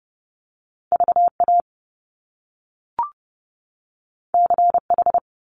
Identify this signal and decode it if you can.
Morse code, heard as a keyed tone, about 30 wpm, 700 Hz. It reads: VA CH
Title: Top Two Letter Combinations in Words 30wpm